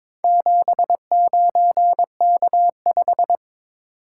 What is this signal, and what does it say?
Morse code, heard as a keyed tone, about 22 wpm, 700 Hz. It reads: 79K5
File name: practice / 22wpm / Four Letter-Number Combinations - 22wpm